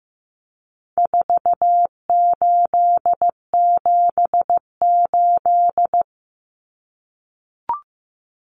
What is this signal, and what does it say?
Morse code, heard as a keyed tone, about 15 wpm, 700 Hz. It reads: 4878
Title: Common Cut Numbers - 4-Digit 15wpm